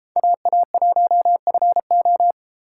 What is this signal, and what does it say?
Morse code, heard as a keyed tone, about 28 wpm, 700 Hz. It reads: AA1FO